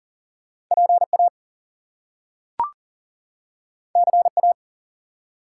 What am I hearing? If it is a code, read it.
Morse code, heard as a keyed tone, about 40 wpm, 700 Hz. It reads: PA CA
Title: US States and Canadian Provinces 40wpm